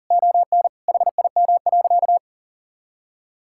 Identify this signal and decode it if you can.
Morse code, heard as a keyed tone, about 40 wpm, 690 Hz. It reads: ON HIM.
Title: Sentences from Top 1000 Words - Review Part 6 - 40wpm